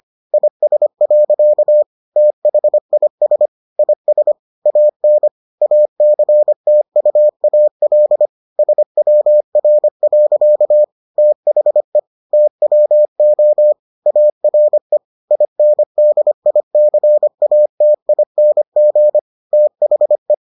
Morse code, heard as a keyed tone, about 25 wpm, 600 Hz. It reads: IS. THIS IS AN ACTUAL SWR. THE TWO ARE INDICATING THE